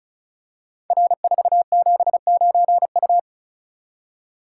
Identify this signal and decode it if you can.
Morse code, heard as a keyed tone, about 36 wpm, 700 Hz. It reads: R479U
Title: Code Groups - 5 Letter:Numbers - 35wpm